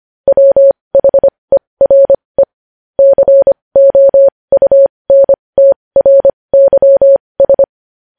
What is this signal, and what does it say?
Morse code, heard as a keyed tone, about 25 wpm, spaced as for 20 wpm, 550 Hz. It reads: WHERE COUNTRYS
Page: news-headlines-in-morse-code-at-20-wpm